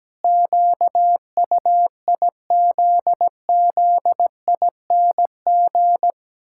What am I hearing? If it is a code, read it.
Morse code, heard as a keyed tone, about 17 wpm, 700 Hz. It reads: QUIZZING
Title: Pangram Challenge v2 - 17wpm